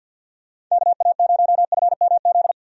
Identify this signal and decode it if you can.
Morse code, heard as a keyed tone, about 53 wpm, 700 Hz. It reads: KA0FMZ
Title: USA Call Signs - 6 Characters - 50wpm